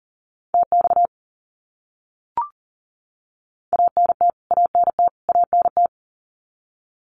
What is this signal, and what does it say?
Morse code, heard as a keyed tone, about 40 wpm, 700 Hz. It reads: TX ANT ANT ANT
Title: Abbreviations Repeated 3x - 40wpm